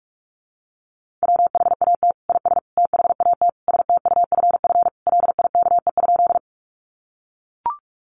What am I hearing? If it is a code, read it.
Morse code, heard as a keyed tone, about 44 wpm, 700 Hz. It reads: WHAT IS THAT STUFF LIKE?
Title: Sentences from Top 1400 Words 45wpm